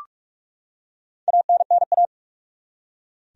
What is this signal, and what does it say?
Morse code, heard as a keyed tone, about 46 wpm, 700 Hz. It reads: ANNA
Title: USA Common Names - Max 4 Letters - 45wpm